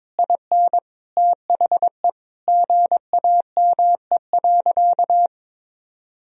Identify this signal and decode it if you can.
Morse code, heard as a keyed tone, about 22 wpm, 700 Hz. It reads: IN THE GAME.
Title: Sentences from Top 1000 Words - Review Part 4 - 22wpm